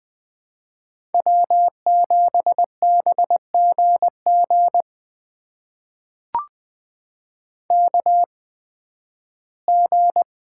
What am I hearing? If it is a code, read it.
Morse code, heard as a keyed tone, about 20 wpm, spaced as for 13 wpm, 700 Hz. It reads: W7BGG KG
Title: USA Call Signs - 5 Characters - 8x Character Spacing - 20wpm